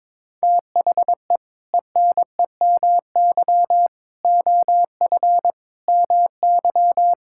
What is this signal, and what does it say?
Morse code, heard as a keyed tone, about 22 wpm, 700 Hz. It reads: THE ENEMY OF MY
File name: practice / 22wpm / Sets of 8 Words - 22wpm